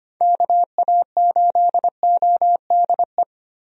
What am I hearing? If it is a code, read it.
Morse code, heard as a keyed tone, about 25 wpm, 700 Hz. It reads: KA8ODE
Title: USA Call Signs - 6 Characters - Repeated 2x - 25wpm